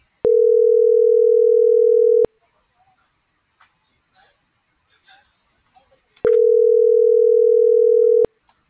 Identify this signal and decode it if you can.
Ringback tone (North America), 440 + 480 Hz.